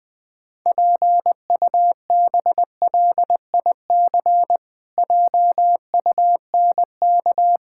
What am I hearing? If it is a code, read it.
Morse code, heard as a keyed tone, about 20 wpm, 700 Hz. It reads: PUBLIC JUNK
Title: Pangram Challenge v2 - 20wpm